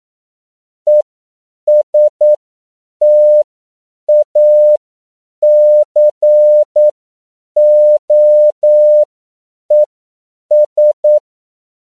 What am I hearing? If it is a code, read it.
Morse code, heard as a keyed tone, about 9 wpm, spaced as for 7 wpm, 600 Hz. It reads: ESTACOES
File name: CW / 10PPM